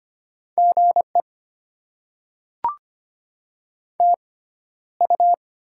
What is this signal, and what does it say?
Morse code, heard as a keyed tone, about 25 wpm, spaced as for 13 wpm, 700 Hz. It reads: GE TU
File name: Top 20 QSO Elements - 6x Character Spacing - 25wpm